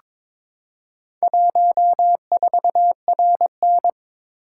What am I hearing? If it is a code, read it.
Morse code, heard as a keyed tone, about 22 wpm, 700 Hz. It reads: 14RN
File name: practice / 22wpm / Four Letter-Number Combinations - 22wpm